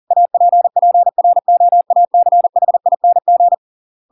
Morse code, heard as a keyed tone, about 40 wpm, 700 Hz. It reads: APPROACHING